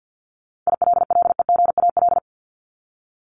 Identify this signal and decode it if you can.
Morse code, heard as a keyed tone, about 48 wpm, 700 Hz. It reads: ILLEGAL
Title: Top 1751-2000 Words 50wpm